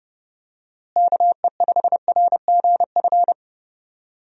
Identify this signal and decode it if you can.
Morse code, heard as a keyed tone, about 30 wpm, 700 Hz. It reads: KE5RGF
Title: USA Call Signs - 6 Characters - Repeated 2x - 30wpm